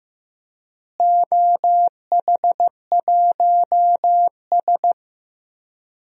Morse code, heard as a keyed tone, about 15 wpm, 700 Hz. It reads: OH1S
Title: DXCC-100 Call Signs - 4 Characters - Repeated 3x - 15wpm